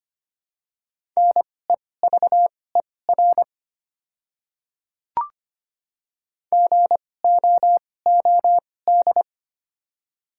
Morse code, heard as a keyed tone, about 25 wpm, spaced as for 18 wpm, 700 Hz. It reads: NEVER GOOD
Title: 2x Character Spacing - Top 500 Words - 25wpm